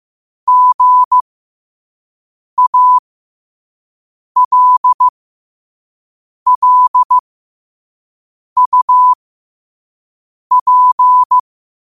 Morse code, heard as a keyed tone, about 15 wpm, spaced as for 5 wpm, 1000 Hz. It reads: GALLUP
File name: qotdfiles / 5wpm